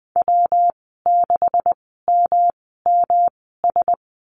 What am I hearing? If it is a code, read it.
Morse code, heard as a keyed tone, about 20 wpm, spaced as for 14 wpm, 700 Hz. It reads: W6MMS
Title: 140 Lesson 23g - Letter f - Call Signs 20wpm